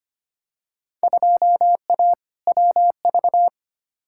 Morse code, heard as a keyed tone, about 25 wpm, 700 Hz. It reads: 2A WV
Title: Field Day Exchanges Encore - 25wpm